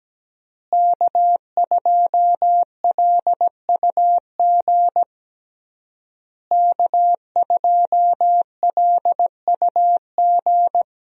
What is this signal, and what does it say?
Morse code, heard as a keyed tone, about 17 wpm, 700 Hz. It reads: K2LUG K2LUG
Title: USA Call Signs - 5 Characters - Repeated 3x - 17wpm